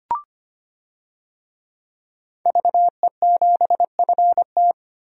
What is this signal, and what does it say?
Morse code, heard as a keyed tone, about 25 wpm, 700 Hz. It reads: VE7FT
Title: DXCC-100 Call Signs - 5 Characters - Repeated 2x - 25wpm